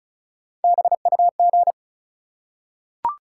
Morse code, heard as a keyed tone, about 36 wpm, 700 Hz. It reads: BUG
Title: Three Letter Words - 35wpm